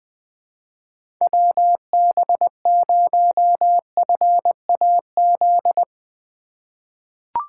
Morse code, heard as a keyed tone, about 21 wpm, 700 Hz. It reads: WB0FAZ
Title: USA Call Signs - 3x Character Spacing - 20wpm